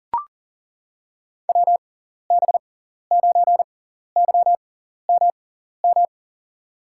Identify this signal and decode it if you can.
Morse code, heard as a keyed tone, about 40 wpm, spaced as for 14 wpm, 700 Hz. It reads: WB9YMM
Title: USA Call Signs - 6 Characters - 6x Character Spacing - 40wpm